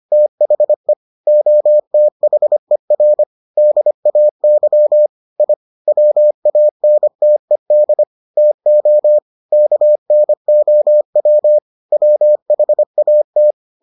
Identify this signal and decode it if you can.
Morse code, heard as a keyed tone, about 25 wpm, 600 Hz. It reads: THE OTHER DAY I WANTED TO KNOW WHAT